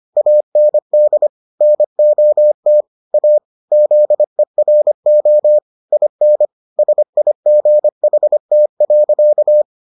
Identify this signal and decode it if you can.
Morse code, heard as a keyed tone, about 25 wpm, 600 Hz. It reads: AND NOT A ZERO IN SIGHT.